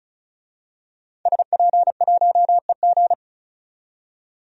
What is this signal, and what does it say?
Morse code, heard as a keyed tone, about 36 wpm, 700 Hz. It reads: SP1EG
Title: DXCC-100 Call Signs - 5 Characters - 35wpm